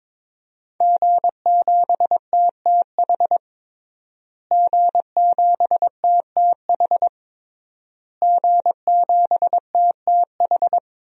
Morse code, heard as a keyed tone, about 21 wpm, 700 Hz. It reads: G7TTH G7TTH G7TTH
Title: DXCC-100 Call Signs - 5 Characters - Repeated 3x - 22wpm